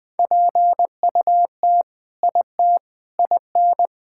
Morse code, heard as a keyed tone, about 20 wpm, 700 Hz. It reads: PUT IT IN
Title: Sentences with 3-Word Phrases - 20wpm